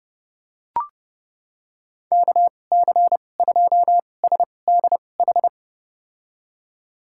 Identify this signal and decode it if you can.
Morse code, heard as a keyed tone, about 30 wpm, spaced as for 22 wpm, 700 Hz. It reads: KC2SDH
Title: USA Call Signs - 2x Character Spacing - 30wpm